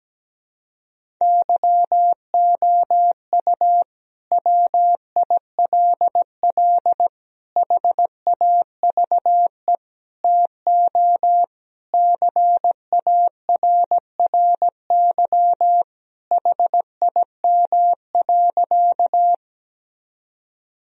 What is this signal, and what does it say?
Morse code, heard as a keyed tone, about 17 wpm, 700 Hz. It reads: YOU WILL HAVE TO CARRY HIM.
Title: Sentences from Top 1000 Words - Review Part 1 - 17wpm